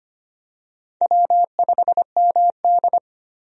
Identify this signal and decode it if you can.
Morse code, heard as a keyed tone, about 25 wpm, 700 Hz. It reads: W5MD
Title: USA Call Signs - 4 Characters - 25wpm